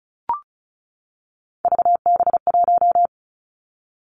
Morse code, heard as a keyed tone, about 36 wpm, 700 Hz. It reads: VB1
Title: Three Challenging Letter-Number Combinations 35wpm